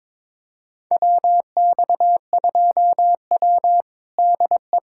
Morse code, heard as a keyed tone, about 22 wpm, 700 Hz. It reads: WX2W DE
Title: USA Callsign DE Callsign - 4 Characters - 22wpm